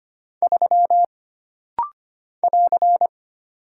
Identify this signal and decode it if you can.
Morse code, heard as a keyed tone, about 25 wpm, 700 Hz. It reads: 3 <AR>
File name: Single Character (Letter:Number:Punctuation:Prosign) - Rapid-Fire - 25wpm